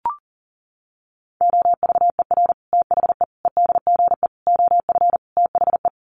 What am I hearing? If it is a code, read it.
Morse code, heard as a keyed tone, about 40 wpm, 700 Hz. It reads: OVER THE EDGE OF THE
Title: Sets of 5 Words Encore2 40wpm